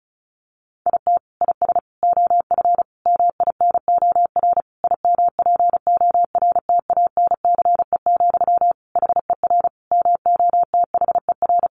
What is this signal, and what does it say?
Morse code, heard as a keyed tone, about 34 wpm, 700 Hz. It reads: IT IS OF MINOR IMPORTANCE, HER MOTHER
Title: Sentences from Top 1500 Words 35wpm